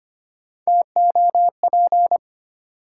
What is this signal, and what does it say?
Morse code, heard as a keyed tone, about 25 wpm, 700 Hz. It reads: TOP
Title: Three Letter Words - 25wpm